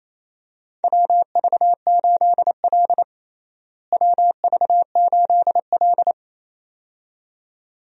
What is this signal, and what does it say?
Morse code, heard as a keyed tone, about 28 wpm, 700 Hz. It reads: WV8L WV8L
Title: USA Call Signs - 4 Characters - Repeated 2x - 28wpm